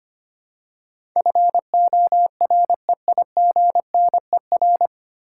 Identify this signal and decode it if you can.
Morse code, heard as a keyed tone, about 25 wpm, 700 Hz. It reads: FOREIGNER
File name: Nine Letter Words - 25wpm